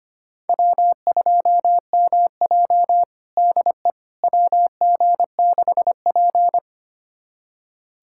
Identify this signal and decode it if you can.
Morse code, heard as a keyed tone, about 25 wpm, 700 Hz. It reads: W2MJ DE WG6P
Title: USA Callsign DE Callsign - 4 Characters - 25wpm